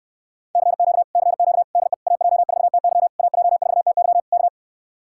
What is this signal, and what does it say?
Morse code, heard as a keyed tone, about 54 wpm, 700 Hz. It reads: CQ CQ DE AJ4TY AJ4TY K
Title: Speed-Racing Common QSO Phrases Encore - 45wpm